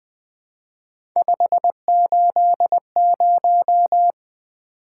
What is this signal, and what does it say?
Morse code, heard as a keyed tone, about 24 wpm, 700 Hz. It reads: 580
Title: Speed-Racing - Three Letter-Number Combinations - 20wpm